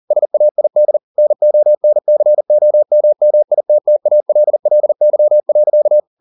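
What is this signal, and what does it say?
Morse code, heard as a keyed tone, about 40 wpm, 600 Hz. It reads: SAID NONCOMMITTALLY.
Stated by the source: Morse code at 40 wpm, practice text, CW med Gnister.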